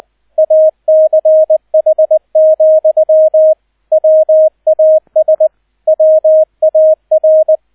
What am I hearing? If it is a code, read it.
Morse code, heard as a keyed tone, about 20 wpm, 630 Hz, with radio noise. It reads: ACH, WAS WAR
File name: Erster Streich 20wpm